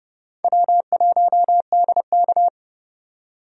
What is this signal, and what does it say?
Morse code, heard as a keyed tone, about 32 wpm, 700 Hz. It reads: W1DK
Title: USA Call Signs - 4 Characters - 30wpm